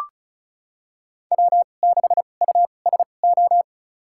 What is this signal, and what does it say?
Morse code, heard as a keyed tone, about 36 wpm, spaced as for 25 wpm, 700 Hz. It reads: W6USO
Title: USA Call Signs - 5 Characters - 2x Character Spacing - 35wpm